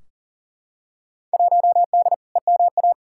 Morse code, heard as a keyed tone, about 40 wpm, 700 Hz. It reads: 1D EMA
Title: Field Day Exchanges - 40wpm